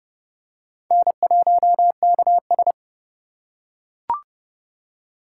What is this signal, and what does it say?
Morse code, heard as a keyed tone, about 30 wpm, 700 Hz. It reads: N1KS
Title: USA Call Signs - 4 Characters - 2x Character Spacing - 30wpm